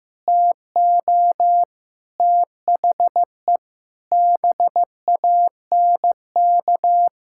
Morse code, heard as a keyed tone, about 16 wpm, 700 Hz. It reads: TO THE BANK
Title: Sentences from Top 1000 Words - Review Part 6 - 15wpm